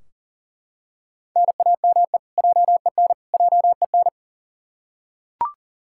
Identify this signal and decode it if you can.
Morse code, heard as a keyed tone, about 40 wpm, 700 Hz. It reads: NAME JEN JEN
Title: K1USN SST QSO Practice 40wpm